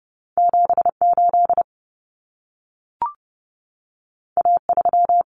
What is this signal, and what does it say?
Morse code, heard as a keyed tone, about 30 wpm, 700 Hz. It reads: 78 A3
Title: Common Cut Numbers - 2-Digit 30wpm